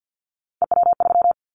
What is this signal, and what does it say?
Morse code, heard as a keyed tone, about 49 wpm, 700 Hz. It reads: EW3
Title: Callsign Prefixes for Top 100 DXCC - 50wpm